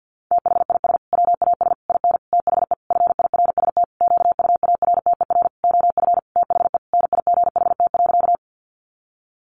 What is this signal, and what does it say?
Morse code, heard as a keyed tone, about 50 wpm, 700 Hz. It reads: THIS WAS IN THE FIRST QUARTER OF THE NIGHT.
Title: Sentences from Top 1300 Words 50wpm